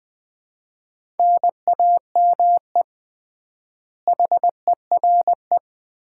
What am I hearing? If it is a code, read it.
Morse code, heard as a keyed tone, about 21 wpm, 700 Hz. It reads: NAME HERE
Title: Common QSO Phrases - 3x Word-Spacing - 20wpm